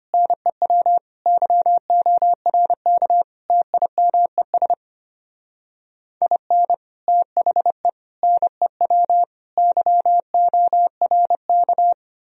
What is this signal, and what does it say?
Morse code, heard as a keyed tone, about 32 wpm, 700 Hz. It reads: NEW YORK TIMES IN THE NEW YORK